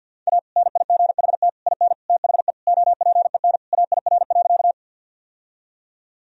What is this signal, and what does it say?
Morse code, heard as a keyed tone, about 47 wpm, 700 Hz. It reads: A NIGHT IN THE OPEN AIR.
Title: Sentences from Top 1000 Words - Review Part 1 - 50wpm